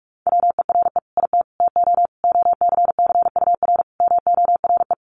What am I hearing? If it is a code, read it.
Morse code, heard as a keyed tone, about 44 wpm, 700 Hz. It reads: WERE IT TO OCCUR MORE
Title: Sentences from Top 1300 Words 45wpm